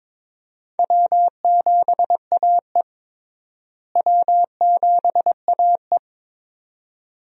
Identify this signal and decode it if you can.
Morse code, heard as a keyed tone, about 22 wpm, 700 Hz. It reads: W7AE W7AE
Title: USA Call Signs - 4 Characters - Repeated 2x - 22wpm